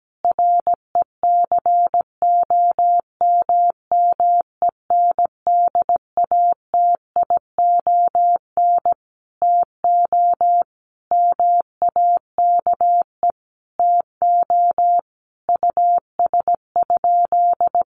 Morse code, heard as a keyed tone, about 17 wpm, 700 Hz. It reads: RECOMMENDATION TO MAKE TO US?
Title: Sentences from Top 1400 Words 17wpm